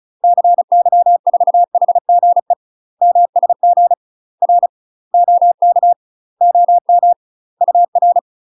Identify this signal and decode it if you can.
Morse code, heard as a keyed tone, about 34 wpm, 700 Hz. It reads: CY4HGE MSG R OK OM UR
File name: UK MASTER QSOs TRAINING / 35WPM / UK MASTER QSOs 0039